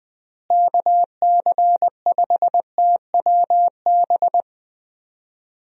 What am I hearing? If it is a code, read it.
Morse code, heard as a keyed tone, about 20 wpm, 700 Hz. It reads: KC5TWB
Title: USA Call Signs - 6 Characters - Repeated 2x - 20wpm